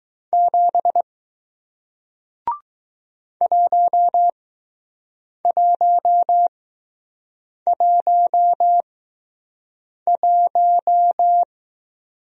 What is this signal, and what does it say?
Morse code, heard as a keyed tone, about 20 wpm, spaced as for 6 wpm, 700 Hz. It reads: 7 1111